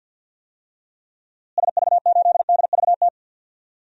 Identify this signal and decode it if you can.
Morse code, heard as a keyed tone, about 46 wpm, 700 Hz. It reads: SV8DVT
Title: DXCC-100 Call Signs - 6 Characters - Repeated 3x - 50wpm